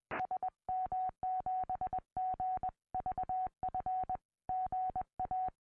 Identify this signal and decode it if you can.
Morse code, heard as a keyed tone, about 20 wpm, 740 Hz. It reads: SM7GVF GA
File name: au / sm7gvf 2m